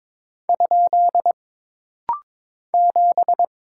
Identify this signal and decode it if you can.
Morse code, heard as a keyed tone, about 22 wpm, 700 Hz. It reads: ? 7
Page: This rapid-fire timing is geared to push your ICR (Instant Character Recognition) skills.